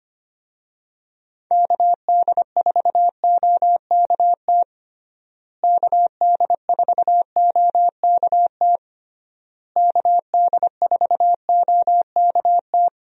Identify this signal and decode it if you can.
Morse code, heard as a keyed tone, about 25 wpm, 700 Hz. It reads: KD4OKT KD4OKT KD4OKT
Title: USA Call Signs - 6 Characters - Repeated 3x - 25wpm